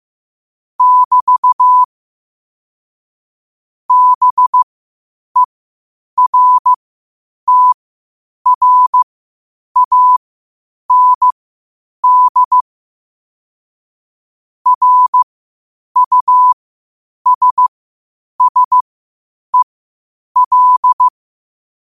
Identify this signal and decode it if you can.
Morse code, heard as a keyed tone, about 15 wpm, spaced as for 8 wpm, 1000 Hz. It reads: <BT> BERTRAND RUSSEL
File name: qotdfiles / 8wpm